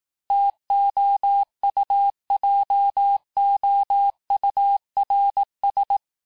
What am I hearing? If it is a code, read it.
Morse code, heard as a keyed tone, about 18 wpm, 800 Hz. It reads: TOUJOURS
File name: cours-du-lundi files / cours-1594--FIN-de-QSO-vitesse-17mpm